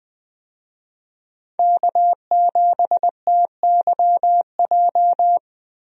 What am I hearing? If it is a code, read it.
Morse code, heard as a keyed tone, about 21 wpm, 700 Hz. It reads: K7TYJ
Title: USA Call Signs - 5 Characters - Repeated 2x - 20wpm